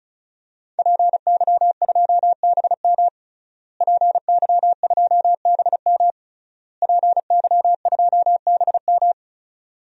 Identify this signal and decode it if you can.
Morse code, heard as a keyed tone, about 34 wpm, 700 Hz. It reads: PY2BM PY2BM PY2BM
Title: DXCC-100 Call Signs - 5 Characters - Repeated 3x - 35wpm